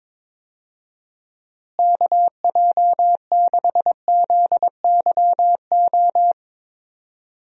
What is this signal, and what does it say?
Morse code, heard as a keyed tone, about 22 wpm, 700 Hz. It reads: KJ6ZYO
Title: USA Call Signs - 6 Characters - Repeated 3x - 22wpm